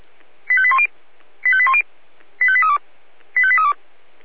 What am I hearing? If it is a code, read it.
Five-tone selective call (ZVEI), heard as quick runs of five tones, tones of about 70 ms. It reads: 85810 85810 85832 85832
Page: SDR/ZVEI